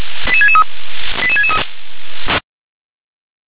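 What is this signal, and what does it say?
Five-tone selective call (ZVEI), heard as quick runs of five tones, tones of about 70 ms. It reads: 88663 88663
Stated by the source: ZVEI five-tone selective call received on SDR.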